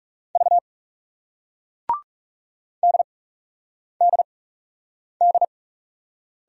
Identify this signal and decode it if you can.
Morse code, heard as a keyed tone, about 40 wpm, 700 Hz. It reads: V D D D